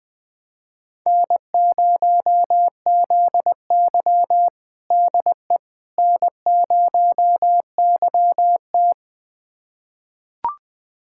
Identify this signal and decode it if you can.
Morse code, heard as a keyed tone, about 20 wpm, 700 Hz. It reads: N0ZY DE N0YT
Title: USA Callsign DE Callsign - 4 Characters - 20wpm